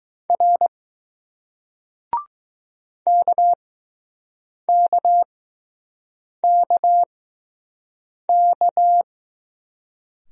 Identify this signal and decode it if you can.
Morse code, heard as a keyed tone, about 20 wpm, spaced as for 6 wpm, 700 Hz. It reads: R KKKK